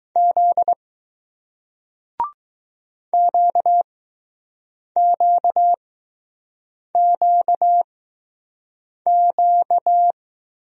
Morse code, heard as a keyed tone, about 20 wpm, spaced as for 6 wpm, 700 Hz. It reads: Z QQQQ